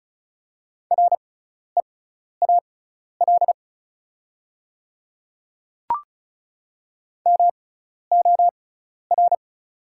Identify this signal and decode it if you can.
Morse code, heard as a keyed tone, about 36 wpm, spaced as for 12 wpm, 690 Hz. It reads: REAL MOR
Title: 6x Character Spacing - Top 300 Words - 35wpm